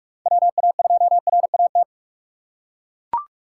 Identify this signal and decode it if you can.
Morse code, heard as a keyed tone, about 43 wpm, 700 Hz. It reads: WA2RAT
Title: USA Call Signs - 2x Character Spacing - 45wpm